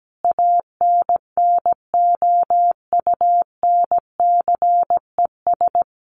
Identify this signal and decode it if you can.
Morse code, heard as a keyed tone, about 17 wpm, 700 Hz. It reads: ANNOUNCES
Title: Additional Easy Phrases 17wpm